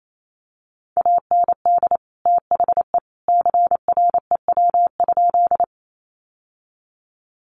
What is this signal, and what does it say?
Morse code, heard as a keyed tone, about 28 wpm, 700 Hz. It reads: AND THE CREW?
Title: Sentences from Top 1500 Words 28wpm